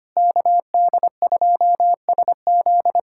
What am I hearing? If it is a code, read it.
Morse code, heard as a keyed tone, about 25 wpm, 700 Hz. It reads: KD2SZ
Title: USA Callsign DE Callsign - 5 Characters - 25wpm